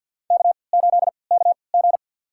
Morse code, heard as a keyed tone, about 51 wpm, spaced as for 28 wpm, 700 Hz. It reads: K8KG
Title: USA Call Signs - 4 Characters - 3x Character Spacing - 50wpm